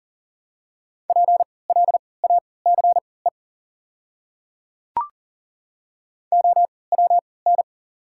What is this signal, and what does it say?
Morse code, heard as a keyed tone, about 40 wpm, spaced as for 23 wpm, 700 Hz. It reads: PLACE OWN
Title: 3x Character Spacing - Top 300 Words - 40wpm